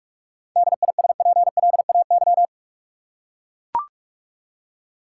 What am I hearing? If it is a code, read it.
Morse code, heard as a keyed tone, about 44 wpm, 700 Hz. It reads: DISPLAY
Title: Seven Letter Words - 45wpm